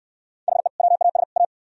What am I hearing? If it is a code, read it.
Morse code, heard as a keyed tone, about 62 wpm, 700 Hz. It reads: HE WAS A